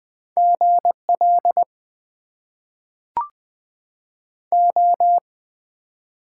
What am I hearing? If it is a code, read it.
Morse code, heard as a keyed tone, about 20 wpm, 700 Hz. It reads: GL O
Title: Top 90 QSO Elements - 6x Character Spacing - 20wpm